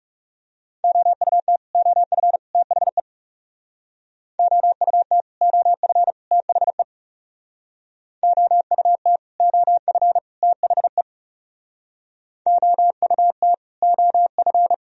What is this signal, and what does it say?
Morse code, heard as a keyed tone, about 39 wpm, 700 Hz. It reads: OUT OF THE OUT OF THE OUT OF THE OUT OF